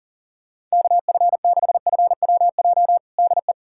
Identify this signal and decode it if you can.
Morse code, heard as a keyed tone, about 40 wpm, 700 Hz. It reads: KF6FWJ DE
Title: USA Callsign DE Callsign - 6 Characters - 40wpm